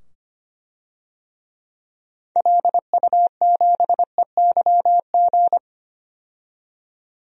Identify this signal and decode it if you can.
Morse code, heard as a keyed tone, about 25 wpm, 700 Hz. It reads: LU7EYG
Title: DXCC-100 Call Signs - 6 Characters - 25wpm